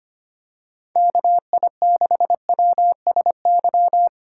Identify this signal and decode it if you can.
Morse code, heard as a keyed tone, about 25 wpm, 700 Hz. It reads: KI6WSY
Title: USA Call Signs - 6 Characters - 4x Character Spacing - 25wpm